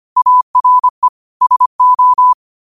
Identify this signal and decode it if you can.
Morse code, heard as a keyed tone, about 25 wpm, 1000 Hz. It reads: ARE SO